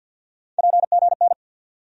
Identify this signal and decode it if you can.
Morse code, heard as a keyed tone, about 52 wpm, 700 Hz. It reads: PGN
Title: USA Call Sign Postfixes - 3 Letters 50wpm